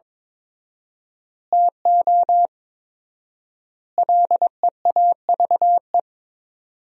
Morse code, heard as a keyed tone, about 22 wpm, 700 Hz. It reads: TO LEAVE
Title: 4x Word-Spacing - Sentences from Top 400 Words - 22wpm